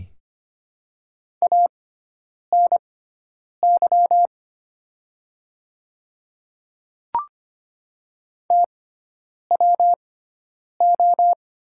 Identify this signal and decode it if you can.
Morse code, heard as a keyed tone, about 25 wpm, spaced as for 9 wpm, 700 Hz. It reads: ANY TWO
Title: 6x Character Spacing - Top 300 Words - 25wpm